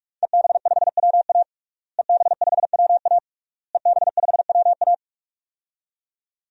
Morse code, heard as a keyed tone, about 41 wpm, 700 Hz. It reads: EB5WA EB5WA EB5WA
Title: DXCC-100 Call Signs - 5 Characters - Repeated 3x - 45wpm